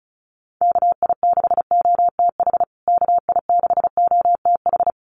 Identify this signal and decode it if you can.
Morse code, heard as a keyed tone, about 34 wpm, 700 Hz. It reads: KI6OTH KI6OTH
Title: USA Call Signs Repeated 2x 35wpm